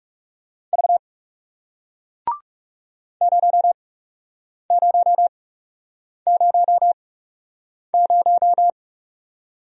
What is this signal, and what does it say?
Morse code, heard as a keyed tone, about 39 wpm, spaced as for 8 wpm, 700 Hz. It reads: V 0000